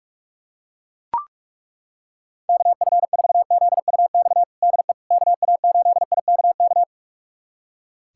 Morse code, heard as a keyed tone, about 41 wpm, 700 Hz. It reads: KF4ZUX DE KA8IKK
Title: USA Callsign DE Callsign - 6 Characters - 45wpm